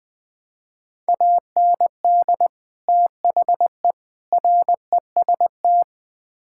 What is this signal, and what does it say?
Morse code, heard as a keyed tone, about 20 wpm, 700 Hz. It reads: AND THE REST
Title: Build A Phrase - 5-Word Phrases - 20wpm